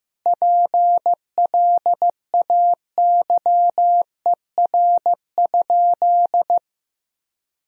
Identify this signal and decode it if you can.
Morse code, heard as a keyed tone, about 15 wpm, 700 Hz. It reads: PLAYER?
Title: Sentences from Top 1000 Words - Review Part 4 - 15wpm